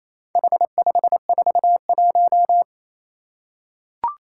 Morse code, heard as a keyed tone, about 28 wpm, 700 Hz. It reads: H541
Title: Four Challenging Letter-Number Combinations 28wpm